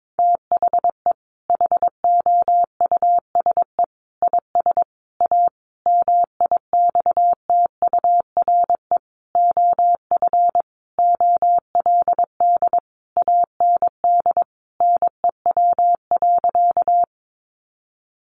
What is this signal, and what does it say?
Morse code, heard as a keyed tone, about 22 wpm, 700 Hz. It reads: THE HOUSE IS A MIXTURE OF OLD AND NEW.
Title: Sentences from Top 1500 Words 22wpm